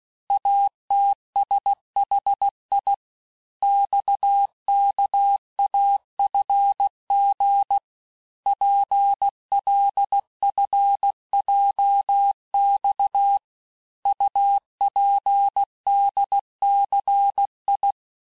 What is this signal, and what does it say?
Morse code, heard as a keyed tone, about 16 wpm, 800 Hz. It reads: ATSHI XKAFG PLFJX UPDCI
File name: cours-du-lundi files / cours-1592--teste-chiffre-vitesse-15mpm